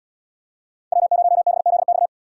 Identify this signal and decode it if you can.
Morse code, heard as a keyed tone, about 65 wpm, 690 Hz. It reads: W0BZX